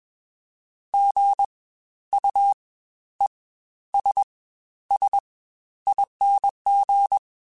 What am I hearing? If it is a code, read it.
Morse code, heard as a keyed tone, about 21 wpm, spaced as for 12 wpm, 800 Hz. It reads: GUESSING
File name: Spaced Alphas